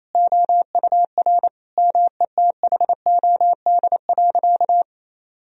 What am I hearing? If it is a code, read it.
Morse code, heard as a keyed tone, about 27 wpm, 700 Hz. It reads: OUR METHOD.
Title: Sentences from Top 1000 Words - Review Part 3 - 28wpm